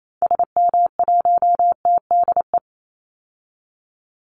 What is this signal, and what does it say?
Morse code, heard as a keyed tone, about 28 wpm, 700 Hz. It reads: SM1TDE
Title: Call Signs Repeated 2x - 28wpm